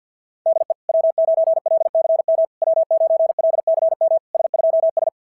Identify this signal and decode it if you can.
Morse code, heard as a keyed tone, about 50 wpm, 650 Hz. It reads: DE W9LCM W9LCM S2S
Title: SOTA QSO Practice 50wpm